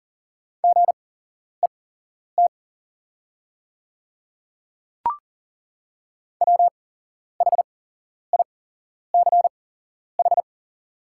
Morse code, heard as a keyed tone, about 40 wpm, spaced as for 11 wpm, 700 Hz. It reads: GET WHICH
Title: 8x Character Spacing - Top 300 Words - 40wpm